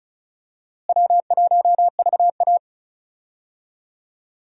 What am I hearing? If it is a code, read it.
Morse code, heard as a keyed tone, about 34 wpm, 700 Hz. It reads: W1VA
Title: USA Call Signs - 4 Characters - Repeated 3x - 35wpm